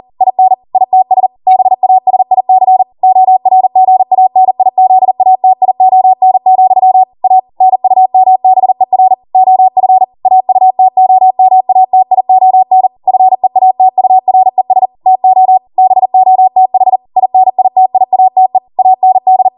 Morse code, heard as a keyed tone, about 40 wpm, 750 Hz. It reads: IN ITS BASIC ORGANIZATION, A NUMBER OF AUTOMATION FEATURES TO BOTH INITIATE AND